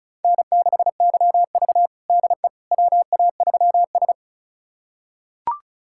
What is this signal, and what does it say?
Morse code, heard as a keyed tone, about 36 wpm, 700 Hz. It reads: N6YV DE WA3S
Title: USA Callsign DE Callsign - 4 Characters - 35wpm